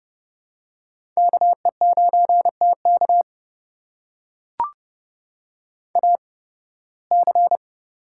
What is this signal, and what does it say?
Morse code, heard as a keyed tone, about 31 wpm, spaced as for 20 wpm, 700 Hz. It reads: KE9TK AC
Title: USA Call Signs - 5 Characters - 8x Character Spacing - 30wpm